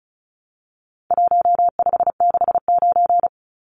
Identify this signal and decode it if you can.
Morse code, heard as a keyed tone, about 34 wpm, 700 Hz. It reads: 1569
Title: Numbers - Contest Style - 4-Digit 35wpm